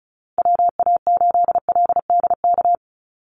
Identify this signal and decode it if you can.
Morse code, heard as a keyed tone, about 34 wpm, 700 Hz. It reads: WA8LDK
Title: USA Call Signs 35wpm